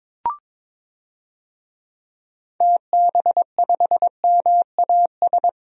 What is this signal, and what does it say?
Morse code, heard as a keyed tone, about 22 wpm, 700 Hz. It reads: TB5MAS
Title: DXCC-100 Call Signs - 6 Characters - 22wpm